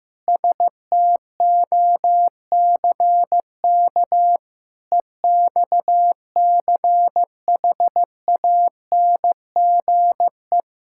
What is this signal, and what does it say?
Morse code, heard as a keyed tone, about 15 wpm, 700 Hz. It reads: STOCK EXCHANGE
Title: Sentences from Top 1000 Words - Review Part 6 - 15wpm